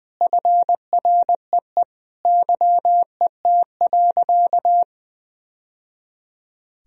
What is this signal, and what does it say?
Morse code, heard as a keyed tone, about 21 wpm, 700 Hz. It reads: FREE YET.
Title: Sentences from Top 1000 Words - Review Part 1 - 20wpm